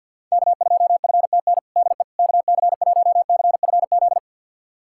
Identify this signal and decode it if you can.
Morse code, heard as a keyed tone, about 50 wpm, 700 Hz. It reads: K2FTN DE KC1CFZ
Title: USA Callsign DE Callsign 50wpm